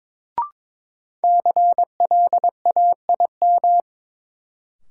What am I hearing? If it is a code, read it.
Morse code, heard as a keyed tone, about 22 wpm, 700 Hz. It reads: CLAIM
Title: Sentences from Top 1500 Words - New Word Drill Alt - 22wpm